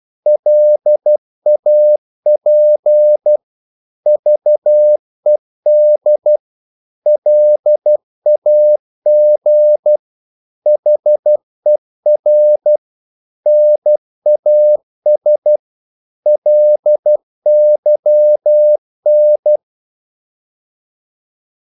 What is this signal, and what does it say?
Morse code, heard as a keyed tone, about 12 wpm, 600 Hz. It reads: LAP VED LAG HER NAS LYN
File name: Korte ord DK 12wpm